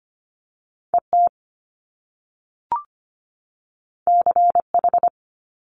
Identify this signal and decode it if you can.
Morse code, heard as a keyed tone, about 25 wpm, 700 Hz. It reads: ET CH
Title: Top Two Letter Combinations in Words 25wpm